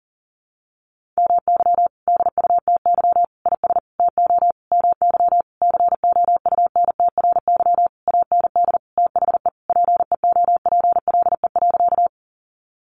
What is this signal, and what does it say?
Morse code, heard as a keyed tone, about 40 wpm, 700 Hz. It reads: MY DUTY IS TO MY COUNTRY AND THE PEOPLE.
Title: Sentences from Top 1500 Words 40wpm